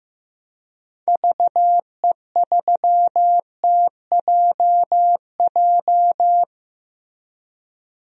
Morse code, heard as a keyed tone, about 15 wpm, 700 Hz. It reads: VE3TJJ
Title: DXCC-100 Call Signs - 6 Characters - Repeated 3x - 15wpm